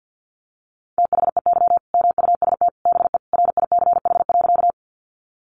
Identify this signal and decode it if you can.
Morse code, heard as a keyed tone, about 50 wpm, 700 Hz. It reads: THEY MUST BE RICH.
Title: Sentences from Top 1100 Words 50wpm